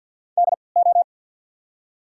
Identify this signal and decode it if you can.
Morse code, heard as a keyed tone, about 54 wpm, spaced as for 29 wpm, 690 Hz. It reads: DO
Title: 3x Character Spacing - Top 100 Words - 50wpm